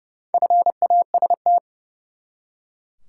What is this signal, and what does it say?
Morse code, heard as a keyed tone, about 30 wpm, 700 Hz. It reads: FAST
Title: Four Letter Words - 30wpm